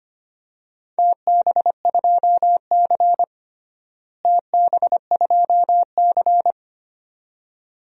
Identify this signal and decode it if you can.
Morse code, heard as a keyed tone, about 25 wpm, 700 Hz. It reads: TB2C TB2C
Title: DXCC-100 Call Signs - 4 Characters - Repeated 2x - 25wpm